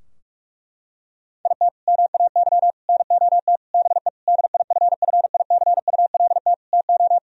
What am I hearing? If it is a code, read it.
Morse code, heard as a keyed tone, about 46 wpm, 700 Hz. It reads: IT MAY NOT BE DIFFICULT TO
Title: Sentences from Top 1000 Words - Review Part 4 - 45wpm